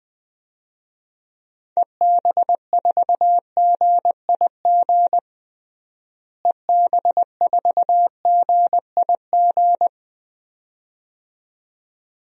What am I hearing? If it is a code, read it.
Morse code, heard as a keyed tone, about 20 wpm, 700 Hz. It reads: EB4GIG EB4GIG
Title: DXCC-100 Call Signs - 6 Characters - Repeated 2x - 20wpm